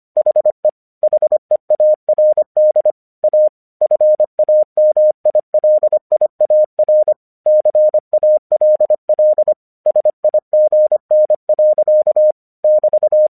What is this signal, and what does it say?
Morse code, heard as a keyed tone, about 25 wpm, 610 Hz. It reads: HE HEARD A FAMILIAR CALL SIGN. <BT>
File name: 1 English 25wpm / AQC29